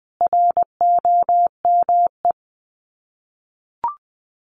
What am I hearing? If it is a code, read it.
Morse code, heard as a keyed tone, about 20 wpm, 700 Hz. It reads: ROME
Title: Common QSO Phrases 20wpm